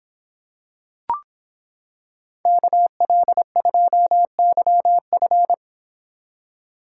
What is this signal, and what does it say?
Morse code, heard as a keyed tone, about 26 wpm, 700 Hz. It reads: KL2YF